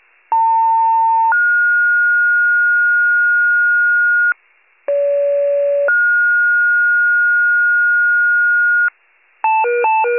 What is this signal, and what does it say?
Two-tone page, heard as a one-second tone, then a three-second tone, about 900 then 1430 Hz, about 570 then 1430 Hz. It is heard from 0.3 to 8.9 s.
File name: wcalert